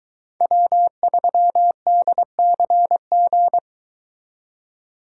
Morse code, heard as a keyed tone, about 23 wpm, 700 Hz. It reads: W3DCG